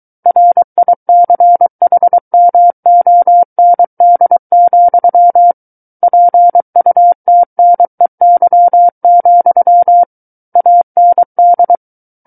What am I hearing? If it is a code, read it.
Morse code, heard as a keyed tone, about 23 wpm, 700 Hz. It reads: RICHMOND, PUTNEY, AND
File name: Book One / 14-Chapter 14 - 23 WPM